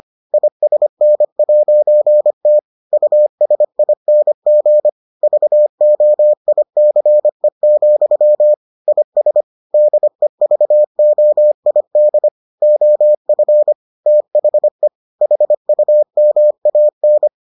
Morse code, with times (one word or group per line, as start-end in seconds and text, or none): 0.3-2.6 ISN'T
2.9-4.9 USING
5.2-8.5 VOICE,
8.9-9.4 IS
9.7-12.3 DEVOID
12.6-13.7 OF
14.1-14.9 THE
15.2-17.3 HUMAN